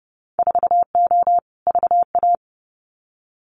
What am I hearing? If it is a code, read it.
Morse code, heard as a keyed tone, about 30 wpm, 700 Hz. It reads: 4O VA
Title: Winter Field Day Exchanges Encore 30wpm